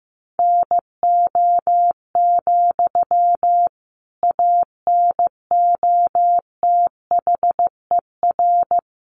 Morse code, heard as a keyed tone, about 15 wpm, 700 Hz. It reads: NO, ANOTHER
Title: Sentences from Top 600 Words 15wpm